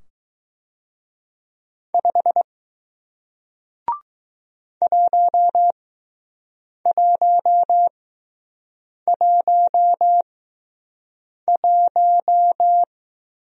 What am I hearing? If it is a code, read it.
Morse code, heard as a keyed tone, about 20 wpm, spaced as for 6 wpm, 700 Hz. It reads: 5 1111